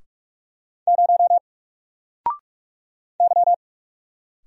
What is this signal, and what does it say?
Morse code, heard as a keyed tone, about 44 wpm, 700 Hz. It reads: 0 Y
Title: Single Character (Letter:Number:Punctuation:Prosign) Alt 45wpm